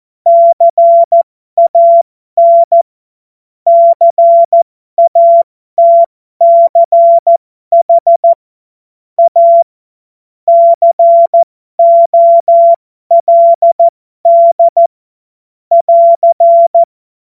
Morse code, heard as a keyed tone, about 14 wpm, spaced as for 12 wpm, 680 Hz. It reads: CAN CATCH A COLD <AR>
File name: Morsekurs / Witze